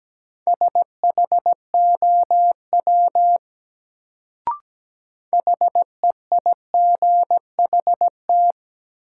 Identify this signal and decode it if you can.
Morse code, heard as a keyed tone, about 17 wpm, 700 Hz. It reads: SHOW HEIGHT
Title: Sets of 5 Words - New Word Drill - 17wpm